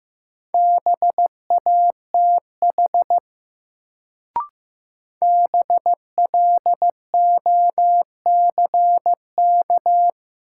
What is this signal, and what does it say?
Morse code, heard as a keyed tone, about 15 wpm, 700 Hz. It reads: BATH BLOCK
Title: Sentences from Top 1600 Words - New Word Drill Alt - 15wpm